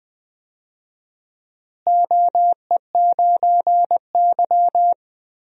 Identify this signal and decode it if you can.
Morse code, heard as a keyed tone, about 20 wpm, 700 Hz. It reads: OE9Y
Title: DXCC-100 Call Signs - 4 Characters - Repeated 2x - 20wpm